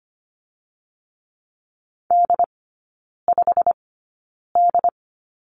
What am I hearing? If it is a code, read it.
Morse code, heard as a keyed tone, about 25 wpm, spaced as for 9 wpm, 690 Hz. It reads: D5D
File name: Lesson 15e - Number 5 - Three Characters 25wpm